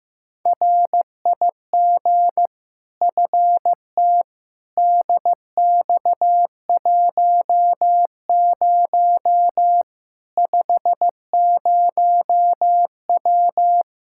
Morse code, heard as a keyed tone, about 15 wpm, 700 Hz. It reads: RIG FT DX10 50W
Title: K1USN SST QSO Practice 15wpm